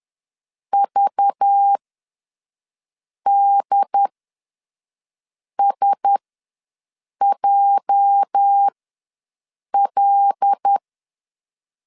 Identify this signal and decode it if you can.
Morse code, heard as a keyed tone, about 10 wpm, spaced as for 5 wpm, 800 Hz. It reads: VDSJL
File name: LEZIONI / 8 YJ45 / 8 YJ ws 10wpm 4043